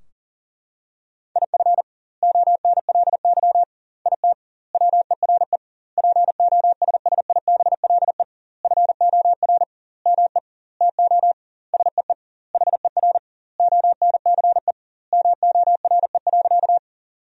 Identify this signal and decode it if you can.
Morse code, heard as a keyed tone, about 40 wpm, 700 Hz. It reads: IF ONLY IT WERE POSSIBLE FOR ME TO SEE HER ONCE MORE.
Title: 2x Word-Spacing - Sentences from Top 500 Words - 40wpm